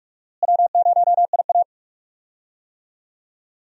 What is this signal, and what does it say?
Morse code, heard as a keyed tone, about 46 wpm, 700 Hz. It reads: W0IA
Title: USA Call Signs - 4 Characters - Repeated 3x - 45wpm